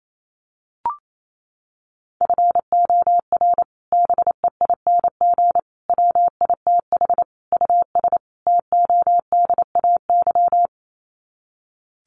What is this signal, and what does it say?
Morse code, heard as a keyed tone, about 28 wpm, 700 Hz. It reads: FOR BEING WITH US TODAY
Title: Sets of 5 Words Encore2 28wpm